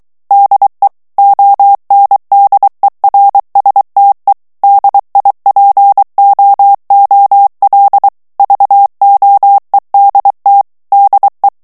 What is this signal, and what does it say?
Morse code, heard as a keyed tone, about 23 wpm, 800 Hz. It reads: DE ONDERSTE DIPOOL VOEDT DE
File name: HOLLANDAIS texte clair 1 vitesse 22 mots minute